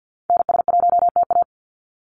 Morse code, heard as a keyed tone, about 50 wpm, 700 Hz. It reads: NS1TA
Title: NA SOTA Activator Callsigns 50wpm